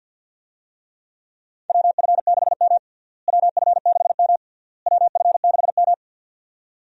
Morse code, heard as a keyed tone, about 47 wpm, 700 Hz. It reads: WF6M WF6M WF6M
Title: USA Call Signs - 4 Characters - Repeated 3x - 50wpm